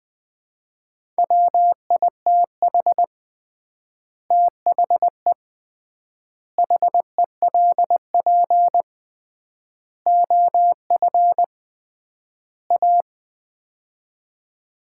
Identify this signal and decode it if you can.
Morse code, heard as a keyed tone, about 20 wpm, 700 Hz. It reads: WITH THE HELP OF A
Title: 3x Word-Spacing - Sets of 5 Words - 20wpm